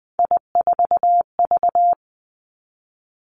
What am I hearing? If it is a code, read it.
Morse code, heard as a keyed tone, about 20 wpm, 700 Hz. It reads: I4V
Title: Three Challenging Letter-Number Combinations 20wpm